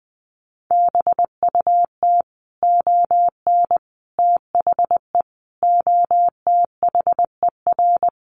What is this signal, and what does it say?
Morse code, heard as a keyed tone, about 20 wpm, 700 Hz. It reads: BUT ON THE OTHER
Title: Sets of 5 Words 20wpm